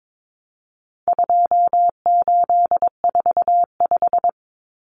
Morse code, heard as a keyed tone, about 22 wpm, 700 Hz. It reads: 2845
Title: Numbers - 4-Digit 22wpm